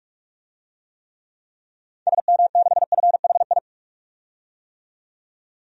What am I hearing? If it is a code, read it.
Morse code, heard as a keyed tone, about 46 wpm, 700 Hz. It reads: SM6FHI
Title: DXCC-100 Call Signs - 6 Characters - 45wpm